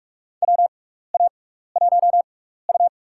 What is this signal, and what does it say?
Morse code, heard as a keyed tone, about 46 wpm, spaced as for 15 wpm, 700 Hz. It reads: WA1U
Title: USA Call Signs - 6x Character Spacing - 45wpm